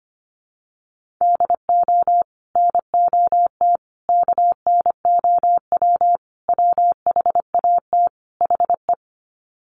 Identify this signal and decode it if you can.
Morse code, heard as a keyed tone, about 25 wpm, 700 Hz. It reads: DO NOT KNOW WHAT HE
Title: Sets of 5 Words Encore2 25wpm